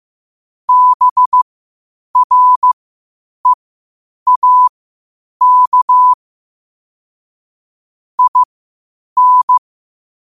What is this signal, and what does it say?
Morse code, heard as a keyed tone, about 15 wpm, spaced as for 8 wpm, 1000 Hz. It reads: BREAK IN
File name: qotdfiles / 8wpm